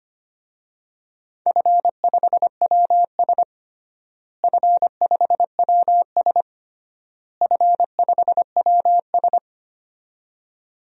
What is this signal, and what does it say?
Morse code, heard as a keyed tone, about 25 wpm, 700 Hz. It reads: F5WS F5WS F5WS
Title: DXCC-100 Call Signs - 4 Characters - Repeated 3x - 25wpm